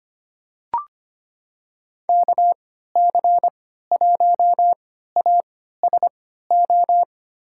Morse code, heard as a keyed tone, about 25 wpm, spaced as for 14 wpm, 700 Hz. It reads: KC1ASO
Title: USA Call Signs - 3x Character Spacing - 25wpm